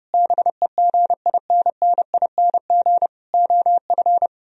Morse code, heard as a keyed tone, about 30 wpm, 700 Hz. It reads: BEGINNING OF